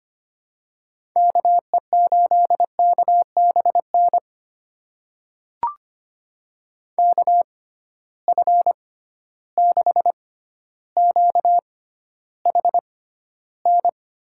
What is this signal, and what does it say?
Morse code, heard as a keyed tone, about 25 wpm, spaced as for 13 wpm, 700 Hz. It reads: KE8KBN KF6QHN
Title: USA Call Signs - 6 Characters - 6x Character Spacing - 25wpm